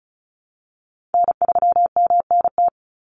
Morse code, heard as a keyed tone, about 34 wpm, 700 Hz. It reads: N3MNT
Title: Call Signs Repeated 2x - 35wpm